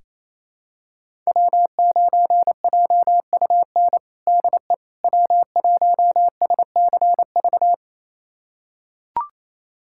Morse code, heard as a keyed tone, about 27 wpm, 700 Hz. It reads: W9JUN DE W1SCV
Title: USA Callsign DE Callsign - 5 Characters - 28wpm